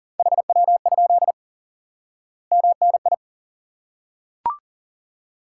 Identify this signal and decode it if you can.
Morse code, heard as a keyed tone, about 40 wpm, 700 Hz. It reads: HW? MNI
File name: IQE Course 3 - Lesson 89 - YAGI - Two Elements - 40wpm